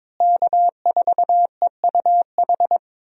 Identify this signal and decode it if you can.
Morse code, heard as a keyed tone, about 22 wpm, 700 Hz. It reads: K4EUH
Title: USA Call Signs - 5 Characters - Repeated 3x - 22wpm